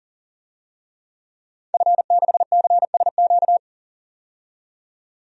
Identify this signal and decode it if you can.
Morse code, heard as a keyed tone, about 38 wpm, 700 Hz. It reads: F6CSQ
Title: DXCC-100 Call Signs - 5 Characters - 40wpm